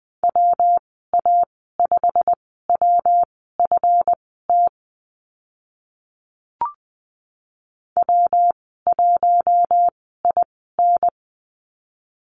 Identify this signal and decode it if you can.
Morse code, heard as a keyed tone, about 20 wpm, spaced as for 14 wpm, 700 Hz. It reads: WA5WFT W1IN
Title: 140 Lesson 23g - Letter f - Call Signs 20wpm